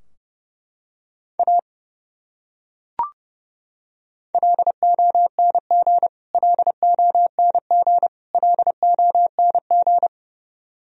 Morse code, heard as a keyed tone, about 30 wpm, 700 Hz. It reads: A LONG LONG LONG
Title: Top 100 Words Repeated 30wpm